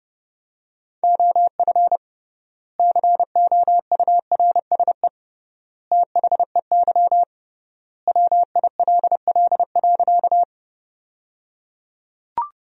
Morse code, heard as a keyed tone, about 30 wpm, 700 Hz. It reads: OF COURSE THEY WILL.
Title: 3x Word-Spacing - Sentences from Top 500 Words - 30wpm